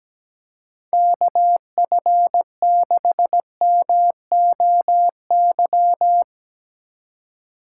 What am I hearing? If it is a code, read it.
Morse code, heard as a keyed tone, about 17 wpm, 700 Hz. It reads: KF6MOY
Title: USA Call Signs - 6 Characters - 3x Character Spacing - 17wpm